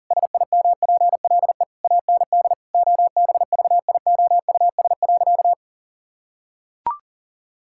Morse code, heard as a keyed tone, about 40 wpm, 700 Hz. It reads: SIMPLE AND OBVIOUS.
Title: Sentences from Top 1600 Words 40wpm